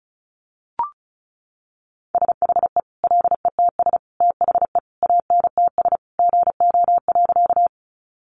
Morse code, heard as a keyed tone, about 34 wpm, 700 Hz. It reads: SHE LETS THE ANTS GO.
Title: Additional Easy Phrases 35wpm